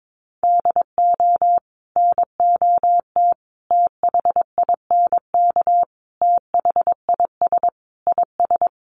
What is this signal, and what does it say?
Morse code, heard as a keyed tone, about 22 wpm, 700 Hz. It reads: DO NOT THINK THIS IS
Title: Sets of 5 Words Encore2 22wpm